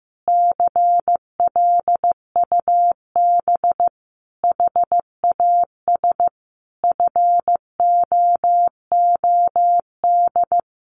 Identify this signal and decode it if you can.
Morse code, heard as a keyed tone, about 15 wpm, 700 Hz. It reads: CLUB HAS FOOD
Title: Sentences from Top 1200 Words 15wpm